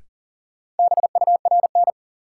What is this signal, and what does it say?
Morse code, heard as a keyed tone, about 40 wpm, 700 Hz. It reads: BURN
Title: Sentences from Top 1300 Words - New Word Drill - 40wpm